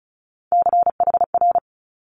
Morse code, heard as a keyed tone, about 36 wpm, 700 Hz. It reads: CHR
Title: Multi-letter Phonemes Quick Fire 35wpm